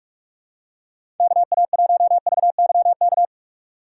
Morse code, heard as a keyed tone, about 46 wpm, 690 Hz. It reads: KA1VYK
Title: USA Call Signs - 6 Characters - 3x Character Spacing - 45wpm